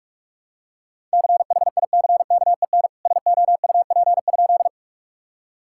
Morse code, heard as a keyed tone, about 46 wpm, 700 Hz. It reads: CHICKEN SOUP?
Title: Sentences from Top 1000 Words - Review Part 6 - 45wpm